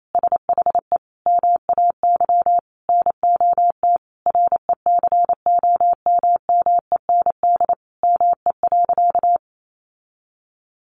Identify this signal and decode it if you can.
Morse code, heard as a keyed tone, about 28 wpm, 700 Hz. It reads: SHE MAY NOT RECOMMEND ME.
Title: Sentences from Top 1200 Words 28wpm